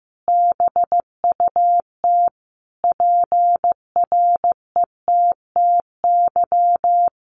Morse code, heard as a keyed tone, about 15 wpm, 700 Hz. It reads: BUT PRETTY
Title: Sentences from Top 1000 Words 15wpm